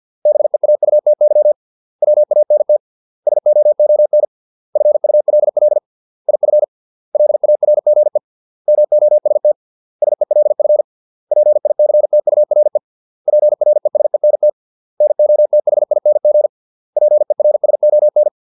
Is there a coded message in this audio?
Yes: Morse code from 0.3 to 18.3 s, heard as a keyed tone, about 50 wpm, 600 Hz. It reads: BEAUTY WANT SOON FULL IF LARGE MOST SELF PICTURE PRESENT NOTHING PERSON